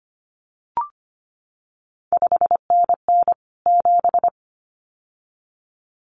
Morse code, heard as a keyed tone, about 25 wpm, 700 Hz. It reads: 5NN 7
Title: CQ WW DX Contest Exchange Encore - 25wpm